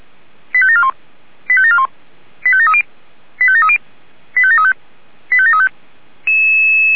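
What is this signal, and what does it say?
Five-tone selective call (ZVEI), heard as quick runs of five tones, tones of about 70 ms. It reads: 85741 85741 85730 85730 85735 85735
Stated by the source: ZVEI five-tone selective call received with an SDR.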